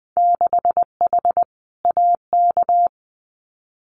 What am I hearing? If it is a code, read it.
Morse code, heard as a keyed tone, about 20 wpm, 700 Hz. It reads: 6H AK
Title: Winter Field Day Exchanges Encore 20wpm